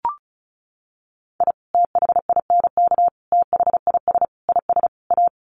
Morse code, heard as a keyed tone, about 34 wpm, 700 Hz. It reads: I THINK THIS IS A
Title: Sets of 5 Words 35wpm